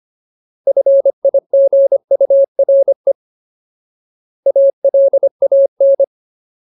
Morse code, heard as a keyed tone, about 25 wpm, 550 Hz. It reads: FIGURE ALAN
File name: Ultimate Word List / 25WPM Ultimate